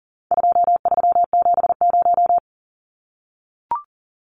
Morse code, heard as a keyed tone, about 40 wpm, 700 Hz. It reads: 2370
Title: Numbers - 4-Digit 40wpm